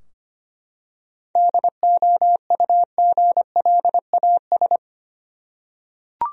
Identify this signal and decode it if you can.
Morse code, heard as a keyed tone, about 25 wpm, 700 Hz. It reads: DOUGLAS
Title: USA Common Names - 25wpm